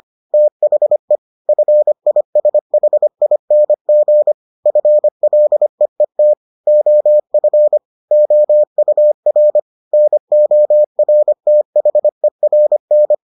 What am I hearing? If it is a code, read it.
Morse code, heard as a keyed tone, about 25 wpm, 600 Hz. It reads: THE FISHING FLEET OF OUR NORTHERN